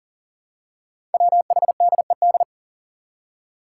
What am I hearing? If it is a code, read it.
Morse code, heard as a keyed tone, about 38 wpm, 700 Hz. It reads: WHDED
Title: Five Challenging Letter-Number Combinations 40wpm